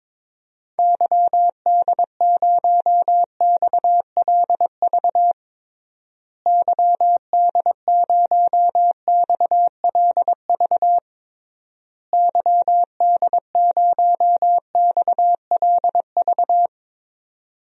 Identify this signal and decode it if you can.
Morse code, heard as a keyed tone, about 22 wpm, 700 Hz. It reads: YD0XLV YD0XLV YD0XLV
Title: DXCC-100 Call Signs - 6 Characters - Repeated 3x - 22wpm